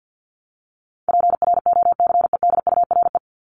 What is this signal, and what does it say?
Morse code, heard as a keyed tone, about 48 wpm, 700 Hz. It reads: PROCEDURE
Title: Top 1751-2000 Words 50wpm